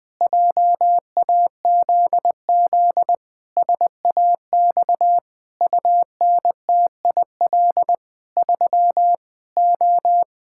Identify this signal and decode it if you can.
Morse code, heard as a keyed tone, about 20 wpm, 700 Hz. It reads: JAZZ SAX UNTIL 3 O
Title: Pangram Challenge v2 - 20wpm